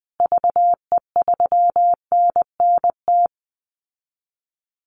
Morse code, heard as a keyed tone, about 20 wpm, 700 Hz. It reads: VE3NNT
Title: Call Signs Repeated 3x - 20wpm